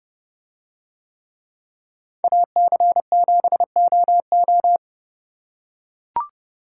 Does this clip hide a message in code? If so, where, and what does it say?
Yes: Morse code from 2.2 to 4.8 s, heard as a keyed tone, about 30 wpm, 700 Hz. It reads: AC7OO